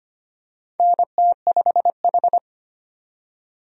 Morse code, heard as a keyed tone, about 25 wpm, 700 Hz. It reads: NT5H
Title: USA Call Signs - 4 Characters - 25wpm